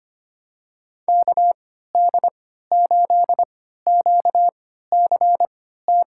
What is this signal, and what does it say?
Morse code, heard as a keyed tone, about 25 wpm, spaced as for 14 wpm, 700 Hz. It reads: KD8QCT
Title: USA Call Signs - 6 Characters - 3x Character Spacing - 25wpm